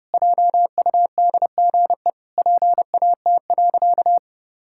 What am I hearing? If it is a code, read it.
Morse code, heard as a keyed tone, about 30 wpm, 700 Hz. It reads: JUDGE PAT.
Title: Pangram Challenge v2 - 30wpm